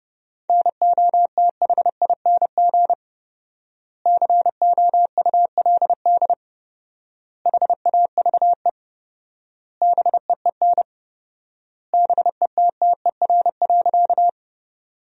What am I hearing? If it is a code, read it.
Morse code, heard as a keyed tone, about 32 wpm, 700 Hz. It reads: NOTHING COULD HAVE BEEN BETTER.
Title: 4x Word-Spacing - Sentences from Top 500 Words - 30wpm